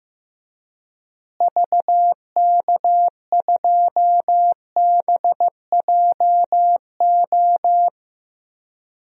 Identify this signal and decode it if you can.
Morse code, heard as a keyed tone, about 15 wpm, 700 Hz. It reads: VK2BJO
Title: DXCC-100 Call Signs - 6 Characters - Repeated 2x - 15wpm